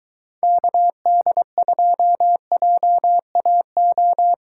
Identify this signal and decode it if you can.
Morse code, heard as a keyed tone, about 22 wpm, 700 Hz. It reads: KD2JAO